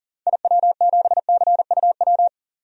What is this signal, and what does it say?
Morse code, heard as a keyed tone, about 40 wpm, 700 Hz. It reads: IW7CUW
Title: DXCC-100 Call Signs - 6 Characters - 40wpm